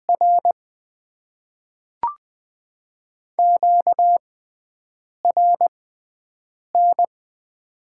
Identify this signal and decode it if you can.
Morse code, heard as a keyed tone, about 20 wpm, spaced as for 7 wpm, 700 Hz. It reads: R QRN
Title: Top 90 QSO Elements - 6x Character Spacing - 20wpm